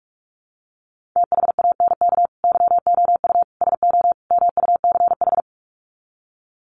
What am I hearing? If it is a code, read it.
Morse code, heard as a keyed tone, about 44 wpm, 700 Hz. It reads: THANK YOU SO MUCH
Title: Sets of 4 Words 45wpm